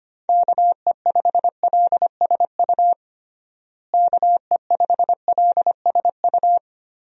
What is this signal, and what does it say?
Morse code, heard as a keyed tone, about 25 wpm, 700 Hz. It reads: KE5LSU KE5LSU
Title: USA Call Signs - 6 Characters - Repeated 2x - 25wpm